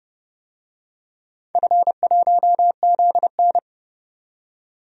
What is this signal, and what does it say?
Morse code, heard as a keyed tone, about 30 wpm, 700 Hz. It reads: F1ZN
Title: DXCC-100 Call Signs - 4 Characters - 30wpm